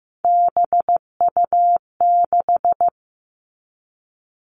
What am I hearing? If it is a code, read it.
Morse code, heard as a keyed tone, about 15 wpm, 700 Hz. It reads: BU6
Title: Three Challenging Letter-Number Combinations 15wpm